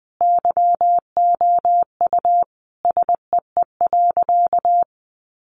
Morse code, heard as a keyed tone, about 20 wpm, 700 Hz. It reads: YOU SEE.
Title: Sentences from Top 600 Words 20wpm